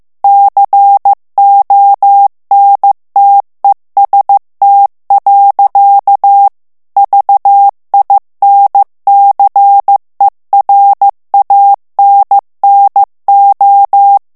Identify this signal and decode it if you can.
Morse code, heard as a keyed tone, about 15 wpm, 800 Hz. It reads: CONTEST. VINCERANNO
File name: morse mp3 / ITALIEN texte clair 2 vitesse 14 mots minute